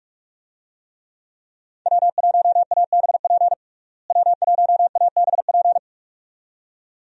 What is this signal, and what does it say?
Morse code, heard as a keyed tone, about 45 wpm, 700 Hz. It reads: W1ABP W1ABP
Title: USA Call Signs - 5 Characters - Repeated 2x - 45wpm